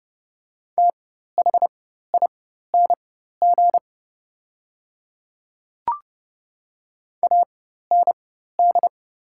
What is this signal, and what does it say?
Morse code, heard as a keyed tone, about 30 wpm, spaced as for 14 wpm, 700 Hz. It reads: THING AND
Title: 4x Character Spacing - Top 100 Words - 30wpm